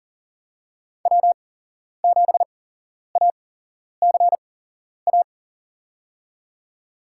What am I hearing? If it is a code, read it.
Morse code, heard as a keyed tone, about 40 wpm, spaced as for 11 wpm, 700 Hz. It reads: W7ACA
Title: USA Call Signs - 5 Characters - 8x Character Spacing - 40wpm